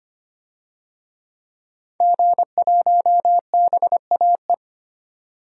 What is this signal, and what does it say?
Morse code, heard as a keyed tone, about 25 wpm, 700 Hz. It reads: G1BAE
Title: DXCC-100 Call Signs - 5 Characters - 25wpm